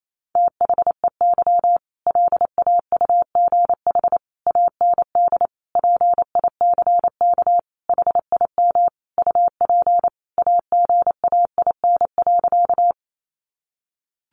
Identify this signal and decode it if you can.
Morse code, heard as a keyed tone, about 28 wpm, 700 Hz. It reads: THEY LAUGH AND PICK HIM UP AGAIN.
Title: Sentences from Top 1000 Words 28wpm